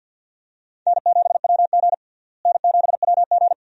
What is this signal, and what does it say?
Morse code, heard as a keyed tone, about 52 wpm, 700 Hz. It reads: N7WG N7WG
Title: USA Call Signs - 4 Characters - Repeated 2x - 50wpm